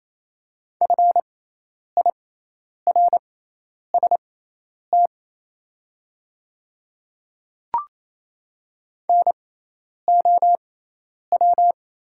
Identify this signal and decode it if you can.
Morse code, heard as a keyed tone, about 28 wpm, spaced as for 10 wpm, 700 Hz. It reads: FIRST NOW
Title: 6x Character Spacing - Top 100 Words - 28wpm